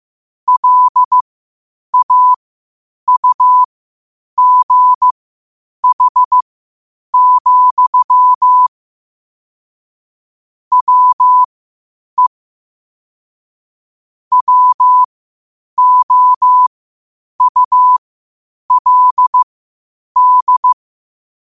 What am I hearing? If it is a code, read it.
Morse code, heard as a keyed tone, about 15 wpm, spaced as for 8 wpm, 1000 Hz. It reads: LAUGH, WE WOULD
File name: qotdfiles / 8wpm